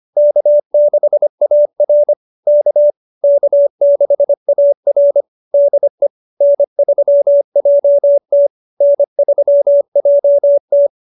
Morse code, heard as a keyed tone, about 25 wpm, 590 Hz. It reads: K6AR K K6AR DE N3JT N3JT